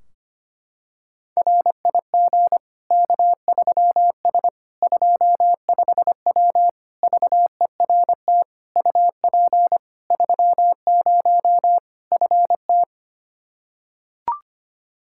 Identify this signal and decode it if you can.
Morse code, heard as a keyed tone, about 25 wpm, 700 Hz. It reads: RIG K3S 25W VERT UP 30 FT
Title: Practice QSO - Exchange 2 - Info - 25wpm